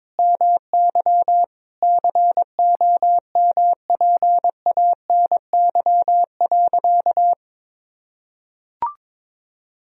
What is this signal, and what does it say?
Morse code, heard as a keyed tone, about 22 wpm, 700 Hz. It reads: MY COMPANY.
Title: Sentences from Top 1000 Words - Review Part 3 - 22wpm